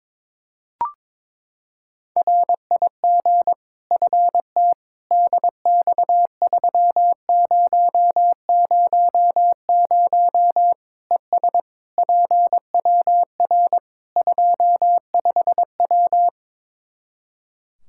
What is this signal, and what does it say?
Morse code, heard as a keyed tone, about 22 wpm, 700 Hz. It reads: RIG FT DX3000 ES PWR 25W
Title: K1USN SST QSO Practice 22wpm